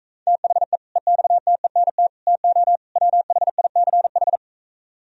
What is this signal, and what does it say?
Morse code, heard as a keyed tone, about 45 wpm, 700 Hz. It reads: THE EXTENT TO WHICH